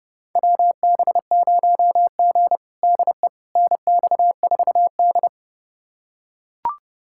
Morse code, heard as a keyed tone, about 30 wpm, 700 Hz. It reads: WB0G DE NX4D
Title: USA Callsign DE Callsign - 4 Characters - 30wpm